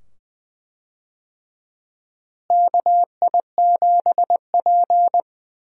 Morse code, heard as a keyed tone, about 20 wpm, 700 Hz. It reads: KI7P
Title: USA Call Signs - 4 Characters - Repeated 2x - 20wpm